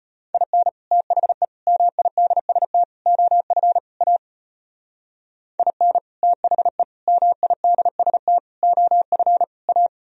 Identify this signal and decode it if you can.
Morse code, heard as a keyed tone, about 32 wpm, 700 Hz. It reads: IN THE MIDST OF A IN THE MIDST OF A